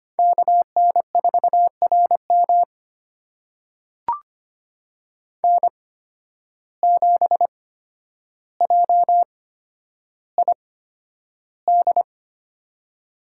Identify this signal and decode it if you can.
Morse code, heard as a keyed tone, about 25 wpm, spaced as for 11 wpm, 700 Hz. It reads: KN4RM N7JID
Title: USA Call Signs - 5 Characters - 8x Character Spacing - 25wpm